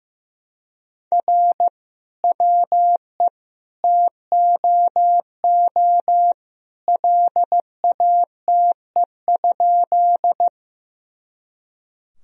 Morse code, heard as a keyed tone, about 15 wpm, 700 Hz. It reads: R WE TOO LATE?
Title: Sentences from Top 300 Words and Abbreviations 15wpm